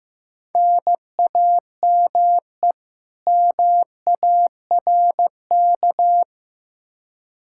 Morse code, heard as a keyed tone, about 15 wpm, 700 Hz. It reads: NAME MARK
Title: Practice QSO - Exchange 1 - Beginning - 15wpm